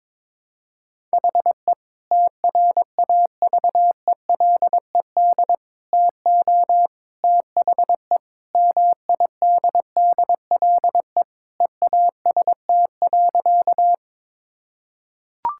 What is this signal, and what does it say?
Morse code, heard as a keyed tone, about 22 wpm, 700 Hz. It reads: HE TRAVELED TO THE MIDDLE EAST.
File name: practice / 22wpm / Sentences with 3-Word Phrases - 22wpm